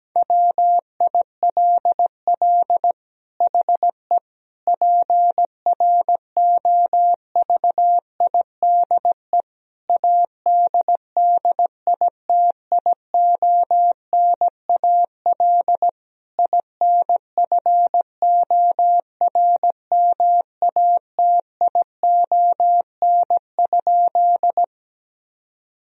Morse code, heard as a keyed tone, about 17 wpm, 700 Hz. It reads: WILL HE PROVIDE ADDITIONAL INFORMATION?
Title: Sentences from Top 1000 Words - Review Part 4 - 17wpm